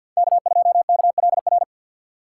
Morse code, heard as a keyed tone, about 51 wpm, 700 Hz. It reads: K2KLR
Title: USA Call Signs - 2x Character Spacing - 50wpm